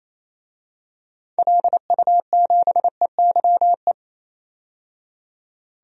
Morse code, heard as a keyed tone, about 28 wpm, 700 Hz. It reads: LU7EYE
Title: DXCC-100 Call Signs - 6 Characters - 28wpm